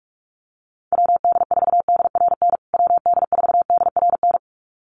Morse code, heard as a keyed tone, about 44 wpm, 700 Hz. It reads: WD4DRN WD4DRN
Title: USA Call Signs Repeated 2x 45wpm